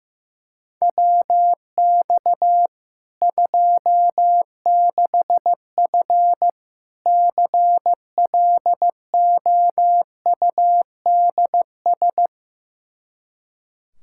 Morse code, heard as a keyed tone, about 15 wpm, 700 Hz. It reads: WX 26F CLOUDS
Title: Practice QSO - Exchange 2 - Info - 15wpm